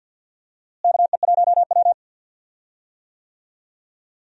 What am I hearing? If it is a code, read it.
Morse code, heard as a keyed tone, about 53 wpm, 700 Hz. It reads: KE1W
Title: USA Call Signs - 4 Characters - Repeated 3x - 50wpm